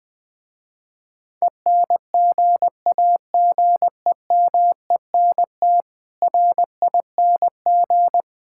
Morse code, heard as a keyed tone, about 20 wpm, 700 Hz. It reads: ENGAGEMENT RING
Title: Difficult Words - Speed-Racing with Hints - 20wpm